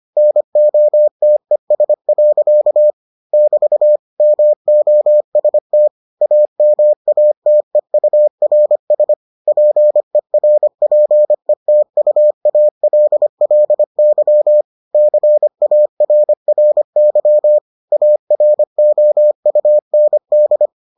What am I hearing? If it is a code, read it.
Morse code, heard as a keyed tone, about 25 wpm, 600 Hz. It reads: NOTES. <BT> MOST AMATEURS PERPETUALLY CARRY AROUND